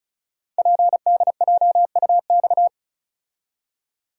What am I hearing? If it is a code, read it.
Morse code, heard as a keyed tone, about 36 wpm, 700 Hz. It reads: PDJUX
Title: Code Groups - 5 Letters - 35wpm